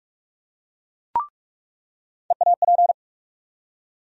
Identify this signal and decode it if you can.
Morse code, heard as a keyed tone, about 47 wpm, 700 Hz. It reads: EAP
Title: USA Call Sign Postfixes - 3 Letters 45wpm